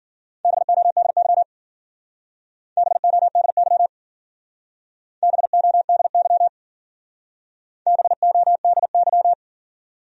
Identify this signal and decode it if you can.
Morse code, heard as a keyed tone, about 53 wpm, 700 Hz. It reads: BODY BODY BODY BODY